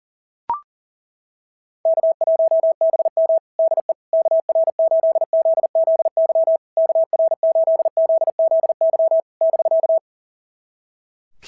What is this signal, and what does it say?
Morse code, heard as a keyed tone, about 37 wpm, 690 Hz. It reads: K1BM DE KR8ZZY KR8ZZY <BK>
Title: SOTA QSO Practice 40wpm